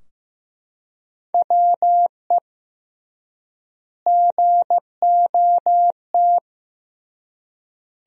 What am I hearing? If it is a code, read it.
Morse code, heard as a keyed tone, about 15 wpm, 700 Hz. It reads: WE GOT
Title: 3x Word-Spacing - Sentences from Top 300 Words - 15wpm